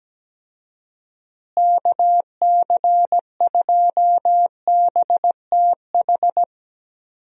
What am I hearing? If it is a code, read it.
Morse code, heard as a keyed tone, about 17 wpm, 700 Hz. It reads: KC2BTH
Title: USA Call Signs - 6 Characters - Repeated 2x - 17wpm